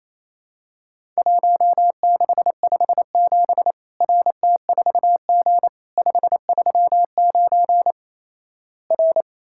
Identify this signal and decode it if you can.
Morse code, heard as a keyed tone, about 28 wpm, 700 Hz. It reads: 1657 RT4G 539 R
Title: Worked All Europe DX Contest - QTC Exchange - 28wpm